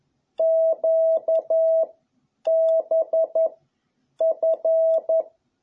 Morse code, heard as a keyed tone, about 11 wpm, 660 Hz. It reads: Q B F
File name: jh8cbh / 2021cw